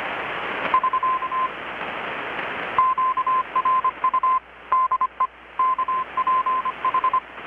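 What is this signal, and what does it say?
Morse code, heard as a keyed tone, about 25 wpm, 1040 Hz. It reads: <SK> QRU DE KPH
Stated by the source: Morse from KPH's Night of Nights XX maritime radiotelegraphy event, received in Chiapas, Mexico about 3,800 km away on a Kenwood R-600 with a grounded 10 m long-wire antenna.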